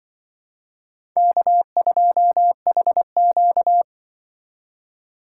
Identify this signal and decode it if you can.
Morse code, heard as a keyed tone, about 25 wpm, 700 Hz. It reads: K2HQ
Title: Speed Racing - USA Call Signs 20wpm